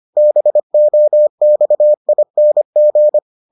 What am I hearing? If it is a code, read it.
Morse code, heard as a keyed tone, about 25 wpm, 600 Hz. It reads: BOXING